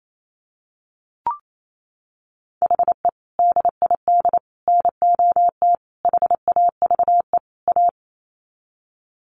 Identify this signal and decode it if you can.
Morse code, heard as a keyed tone, about 28 wpm, 700 Hz. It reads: HE DID NOT HAVE A
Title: Sets of 5 Words Encore 28wpm